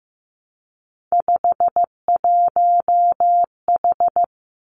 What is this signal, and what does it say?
Morse code, heard as a keyed tone, about 15 wpm, 700 Hz. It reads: 51H
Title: Three Challenging Letter-Number Combinations 15wpm